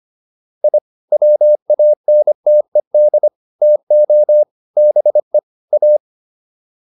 Morse code, heard as a keyed tone, about 25 wpm, 600 Hz. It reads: I WANTED TO BE A